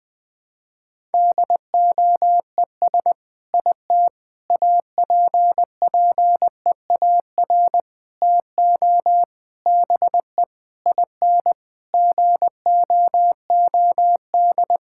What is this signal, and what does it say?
Morse code, heard as a keyed tone, about 20 wpm, 700 Hz. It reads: DOES IT APPEAR TO BE IN GOOD
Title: Sentences from Top 1000 Words - Review Part 6 - 20wpm